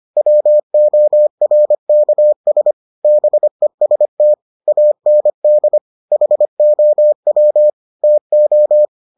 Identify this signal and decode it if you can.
Morse code, heard as a keyed tone, about 25 wpm, 600 Hz. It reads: WORKS BEST AND HOW TO